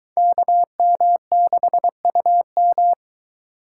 Morse code, heard as a keyed tone, about 22 wpm, 700 Hz. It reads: KM6UM